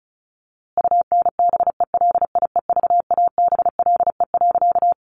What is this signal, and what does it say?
Morse code, heard as a keyed tone, about 34 wpm, 700 Hz. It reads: UNBELIEVABLE.
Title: A Sound Of Thunder 35wpm